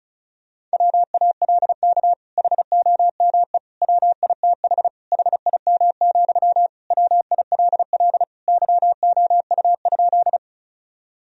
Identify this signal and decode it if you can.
Morse code, heard as a keyed tone, about 36 wpm, 690 Hz. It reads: WALK HOME WITH HIM, WILL YOU?
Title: Sentences from Top 1000 Words - Review Part 1 - 35wpm